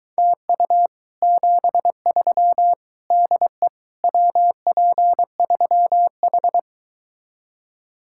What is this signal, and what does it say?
Morse code, heard as a keyed tone, about 23 wpm, 700 Hz. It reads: TU 73 DE WP3H